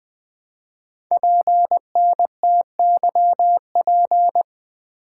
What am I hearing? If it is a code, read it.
Morse code, heard as a keyed tone, about 20 wpm, 700 Hz. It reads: PNTYP
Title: Code Groups - 5 Letter:Numbers - 20wpm